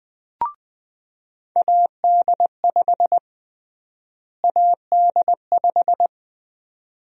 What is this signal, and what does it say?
Morse code, heard as a keyed tone, about 20 wpm, 700 Hz. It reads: AD5 AD5
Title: USA Call Sign Prefixes - Repeated 2x - 20wpm